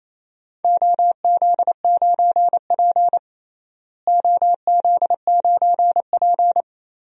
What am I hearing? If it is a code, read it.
Morse code, heard as a keyed tone, about 27 wpm, 700 Hz. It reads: OZ9P OZ9P
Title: DXCC-100 Call Signs - 4 Characters - Repeated 3x - 28wpm